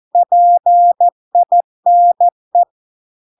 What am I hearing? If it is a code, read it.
Morse code, heard as a keyed tone, about 14 wpm, 700 Hz. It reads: PINE